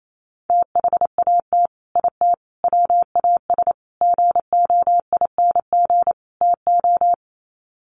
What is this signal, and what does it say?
Morse code, heard as a keyed tone, about 28 wpm, 700 Hz. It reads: THAT IT WAS GOING TO
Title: Sets of 5 Words Encore2 28wpm